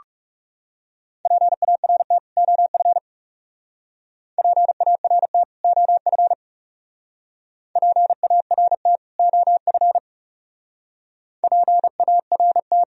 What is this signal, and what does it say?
Morse code, heard as a keyed tone, about 38 wpm, 700 Hz. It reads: PART OF PART OF PART OF PART